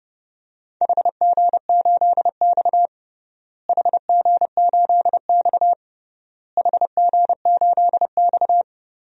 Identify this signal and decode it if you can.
Morse code, heard as a keyed tone, about 30 wpm, 700 Hz. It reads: HG8X HG8X HG8X
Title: DXCC-100 Call Signs - 4 Characters - Repeated 3x - 30wpm